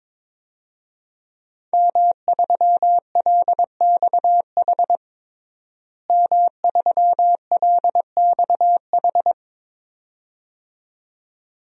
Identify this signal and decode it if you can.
Morse code, heard as a keyed tone, about 22 wpm, 700 Hz. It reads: M3LXH M3LXH
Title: DXCC-100 Call Signs - 5 Characters - Repeated 2x - 22wpm